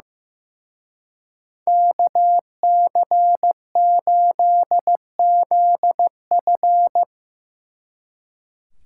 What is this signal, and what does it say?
Morse code, heard as a keyed tone, about 17 wpm, 700 Hz. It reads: KC8ZF
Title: Speed Racing - USA Call Signs 15wpm